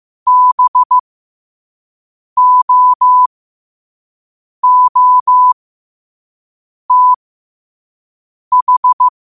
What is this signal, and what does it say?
Morse code, heard as a keyed tone, about 15 wpm, spaced as for 5 wpm, 1000 Hz. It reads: BOOTH
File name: qotdfiles / 5wpm